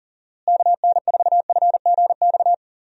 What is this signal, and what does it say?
Morse code, heard as a keyed tone, about 40 wpm, 700 Hz. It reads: KN4FGX
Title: USA Call Signs - 6 Characters - Repeated 2x - 40wpm